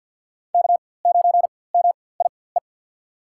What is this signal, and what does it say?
Morse code, heard as a keyed tone, about 47 wpm, spaced as for 23 wpm, 700 Hz. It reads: K9MIE
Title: USA Call Signs - 4x Character Spacing - 50wpm